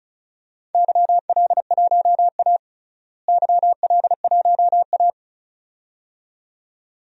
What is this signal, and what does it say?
Morse code, heard as a keyed tone, about 36 wpm, 700 Hz. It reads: YL1A YL1A
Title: DXCC-100 Call Signs - 4 Characters - Repeated 2x - 35wpm